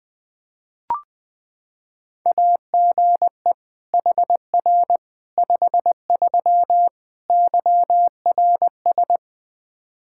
Morse code, heard as a keyed tone, about 20 wpm, 700 Hz. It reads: AGE HR 53 YRS
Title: Practice QSO - Exchange 3 - More Info - 20wpm